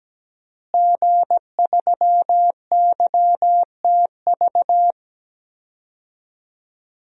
Morse code, heard as a keyed tone, about 17 wpm, 700 Hz. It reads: G3YTV
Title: DXCC-100 Call Signs - 5 Characters - Repeated 3x - 17wpm